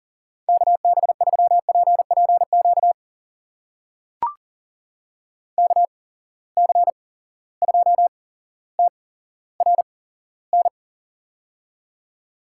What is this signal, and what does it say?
Morse code, heard as a keyed tone, about 39 wpm, spaced as for 17 wpm, 700 Hz. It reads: KB3PPQ KC2TRN
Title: USA Call Signs - 6 Characters - 8x Character Spacing - 40wpm